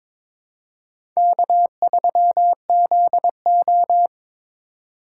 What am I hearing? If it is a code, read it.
Morse code, heard as a keyed tone, about 21 wpm, 700 Hz. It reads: K3ZO
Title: USA Call Signs - 4 Characters - 3x Character Spacing - 22wpm